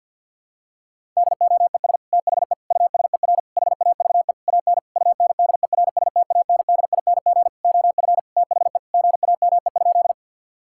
Morse code, heard as a keyed tone, about 50 wpm, 700 Hz. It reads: DOES THE USER HAVE AN UNDERSTANDING OF THE GAME?
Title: Sentences from Top 1000 Words - Review Part 5 - 50wpm